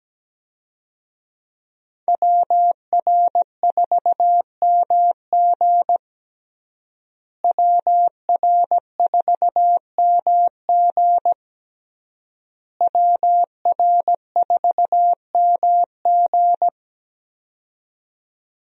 Morse code, heard as a keyed tone, about 17 wpm, 700 Hz. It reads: WR4MG WR4MG WR4MG
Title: USA Call Signs - 5 Characters - Repeated 3x - 17wpm